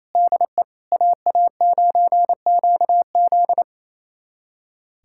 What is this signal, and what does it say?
Morse code, heard as a keyed tone, about 27 wpm, 700 Hz. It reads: DE AA9QZ
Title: USA Callsign DE Callsign - 5 Characters - 28wpm